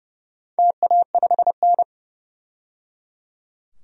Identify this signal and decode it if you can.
Morse code, heard as a keyed tone, about 30 wpm, 700 Hz. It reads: TA5N
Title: DXCC-100 Call Signs - 4 Characters - 30wpm